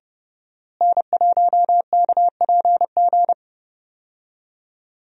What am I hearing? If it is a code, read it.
Morse code, heard as a keyed tone, about 32 wpm, 700 Hz. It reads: N1KPG
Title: USA Call Signs - 5 Characters - Repeated 2x - 30wpm